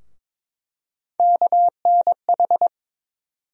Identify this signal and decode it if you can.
Morse code, heard as a keyed tone, about 22 wpm, 700 Hz. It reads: KNH
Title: USA Call Sign Postfixes - 3 Letters 22wpm